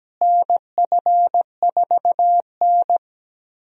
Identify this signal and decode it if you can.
Morse code, heard as a keyed tone, about 17 wpm, 700 Hz. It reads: NF4N
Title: USA Call Signs - 4 Characters - Repeated 2x - 17wpm